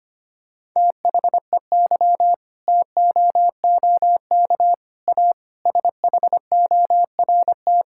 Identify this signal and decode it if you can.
Morse code, heard as a keyed tone, about 25 wpm, 700 Hz. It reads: THEY TOOK A SHORT
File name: practice / 25wpm / Difficult Sentences with Hints - 25wpm